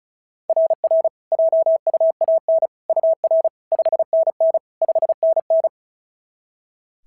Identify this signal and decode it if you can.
Morse code, heard as a keyed tone, about 36 wpm, 690 Hz. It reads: RR JUAN UR 5NN 5NN
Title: SOTA QSO Practice 35wpm